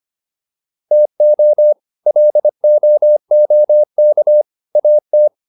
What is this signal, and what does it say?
Morse code, heard as a keyed tone, about 25 wpm, 600 Hz. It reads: TO LOOK AT